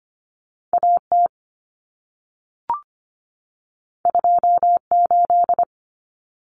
Morse code, heard as a keyed tone, about 25 wpm, 700 Hz. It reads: AT 28
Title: Common Cut Numbers - 2-Digit 25wpm